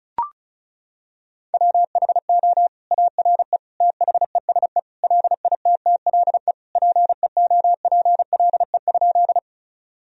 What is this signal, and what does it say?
Morse code, heard as a keyed tone, about 36 wpm, 700 Hz. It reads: WHO ARE THESE LITTLE PEOPLE?
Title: Sentences from Top 1000 Words - Review Part 1 - 35wpm